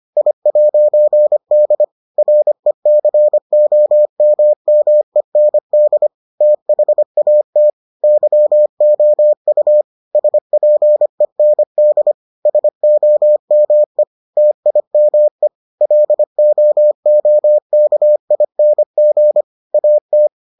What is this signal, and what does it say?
Morse code, heard as a keyed tone, about 25 wpm, 600 Hz. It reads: I'D RECOMMEND THAT YOU SPEND SOME TIME LOOKING AT